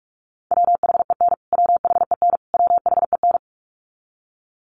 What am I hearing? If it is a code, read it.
Morse code, heard as a keyed tone, about 44 wpm, 700 Hz. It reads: WHEN WHEN WHEN
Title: Top 500 Words Repeated 45wpm